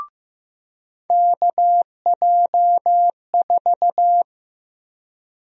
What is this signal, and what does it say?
Morse code, heard as a keyed tone, about 15 wpm, 700 Hz. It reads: KJ4
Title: USA Call Sign Prefixes - 15wpm